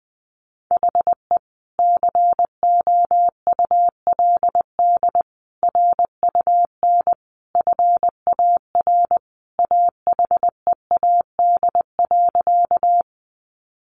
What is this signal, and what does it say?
Morse code, heard as a keyed tone, about 20 wpm, 700 Hz. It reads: HE COULD RUN FAR AHEAD.
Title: Sentences from Top 1100 Words 20wpm